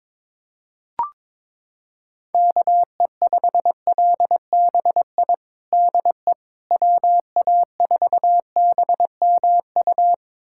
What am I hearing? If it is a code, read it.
Morse code, heard as a keyed tone, about 22 wpm, 700 Hz. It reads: KE5LBI DE WA4BMU
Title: USA Callsign DE Callsign - 6 Characters - 22wpm